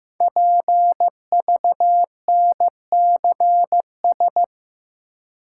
Morse code, heard as a keyed tone, about 15 wpm, 700 Hz. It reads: PVNCS
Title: Code Groups - 5 Letter:Numbers - 15wpm